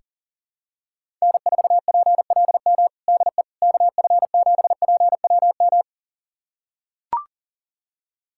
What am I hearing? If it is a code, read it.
Morse code, heard as a keyed tone, about 37 wpm, 700 Hz. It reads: N4PLM DE KF7PWM
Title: USA Callsign DE Callsign 40wpm